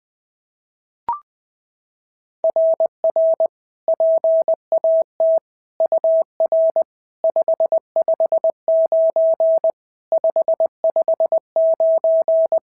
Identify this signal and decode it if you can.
Morse code, heard as a keyed tone, about 20 wpm, 690 Hz. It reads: RR PAT UR 559 559
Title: SOTA QSO Practice 20wpm